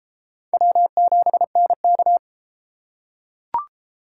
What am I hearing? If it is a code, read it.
Morse code, heard as a keyed tone, about 33 wpm, 700 Hz. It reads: W7NK